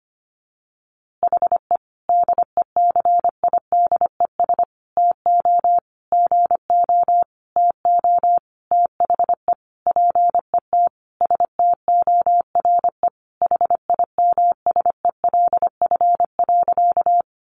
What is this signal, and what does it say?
Morse code, heard as a keyed tone, about 25 wpm, 700 Hz. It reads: HE DECIDES TO GO TO THE PET STORE HIMSELF.
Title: Easy Phrases 25wpm